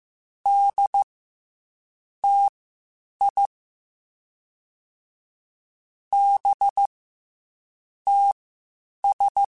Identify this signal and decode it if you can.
Morse code, heard as a keyed tone, about 15 wpm, spaced as for 7 wpm, 800 Hz. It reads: DTI BTS
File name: CW Comparisons / CW Comparisons-mp3s / CW Comparisons Broken Characters @13